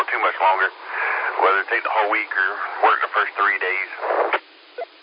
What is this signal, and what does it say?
Emergency Alert System (SAME) tones, heard as harsh buzzing bursts, with speech. It is heard from 0.9 to 3.9 s.